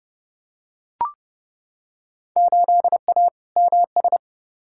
Morse code, heard as a keyed tone, about 30 wpm, 700 Hz. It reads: 8A MS
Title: Field Day Exchanges - 30wpm